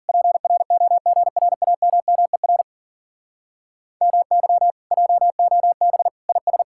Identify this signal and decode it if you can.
Morse code, heard as a keyed tone, about 51 wpm, 700 Hz. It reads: PROGRAMMER MY JOB IS